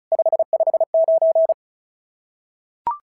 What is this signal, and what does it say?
Morse code, heard as a keyed tone, about 36 wpm, 690 Hz. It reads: 559
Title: SOTA QSO Practice 35wpm